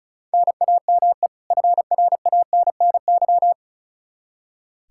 Morse code, heard as a keyed tone, about 36 wpm, 700 Hz. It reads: NAME FRANNY
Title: Practice QSO - Exchange 1 - Beginning - 35wpm